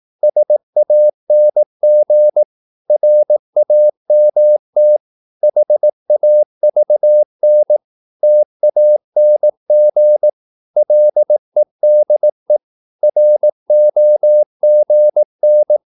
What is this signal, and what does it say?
Morse code, heard as a keyed tone, about 18 wpm, 600 Hz. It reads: SANG RAMT HAVN TANG LEDE ROGN
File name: Korte ord DK 18wpm